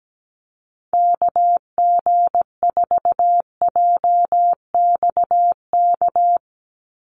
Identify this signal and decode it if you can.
Morse code, heard as a keyed tone, about 17 wpm, 700 Hz. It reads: KG4JXK
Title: USA Call Signs 17wpm